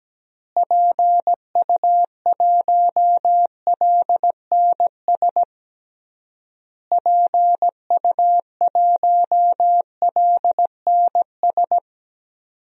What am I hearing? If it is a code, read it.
Morse code, heard as a keyed tone, about 17 wpm, 700 Hz. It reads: PU1LNS PU1LNS
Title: DXCC-100 Call Signs - 6 Characters - Repeated 3x - 17wpm